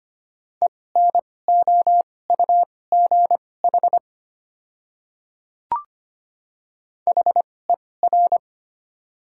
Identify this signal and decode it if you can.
Morse code, heard as a keyed tone, about 25 wpm, spaced as for 18 wpm, 700 Hz. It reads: ENOUGH HER
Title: 2x Character Spacing - Top 500 Words - 25wpm